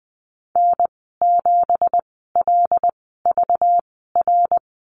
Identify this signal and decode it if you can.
Morse code, heard as a keyed tone, about 20 wpm, spaced as for 14 wpm, 700 Hz. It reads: N7LVR
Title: 193 Lesson 31g - Letter v - Call Signs 20wpm